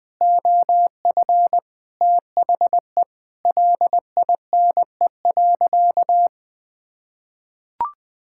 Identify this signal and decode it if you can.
Morse code, heard as a keyed tone, about 20 wpm, 700 Hz. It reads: OF THE LINE.
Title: Sentences with 5-Word Phrases - Spoken Beginning - 20wpm